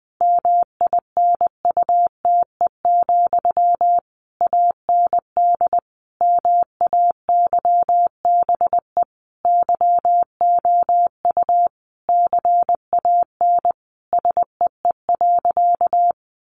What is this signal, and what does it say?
Morse code, heard as a keyed tone, about 20 wpm, 700 Hz. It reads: MINUTE, AND MAYBE YOU CAN SEE.
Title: Sentences from Top 1000 Words 20wpm